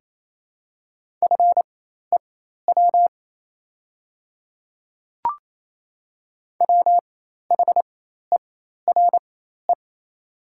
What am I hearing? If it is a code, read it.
Morse code, heard as a keyed tone, about 28 wpm, spaced as for 13 wpm, 690 Hz. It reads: FEW WHERE
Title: 4x Character Spacing - Top 300 Words - 28wpm